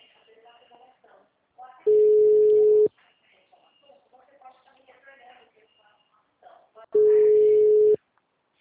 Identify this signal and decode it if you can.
Ringback tone (Europe, 425 Hz), with speech.
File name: out- -2012- - - .61659